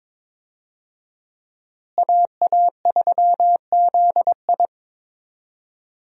Morse code, heard as a keyed tone, about 22 wpm, 700 Hz. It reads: AA3ZI
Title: USA Call Signs - 5 Characters - Repeated 3x - 22wpm